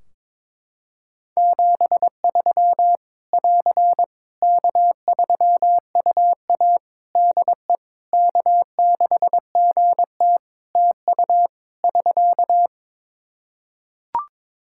Morse code, heard as a keyed tone, about 22 wpm, 700 Hz. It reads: 73 <AR> K3UA DE K6GT TU <SK>
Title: Practice QSO - Exchange 4 - Ending - 22wpm